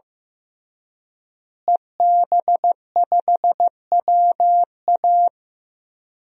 Morse code, heard as a keyed tone, about 15 wpm, 700 Hz. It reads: EB5WA
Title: DXCC-100 Call Signs - 5 Characters - Repeated 3x - 15wpm